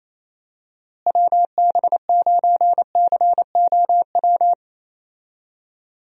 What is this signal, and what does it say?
Morse code, heard as a keyed tone, about 28 wpm, 700 Hz. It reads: WB9COW
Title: USA Call Signs - 6 Characters - Repeated 2x - 28wpm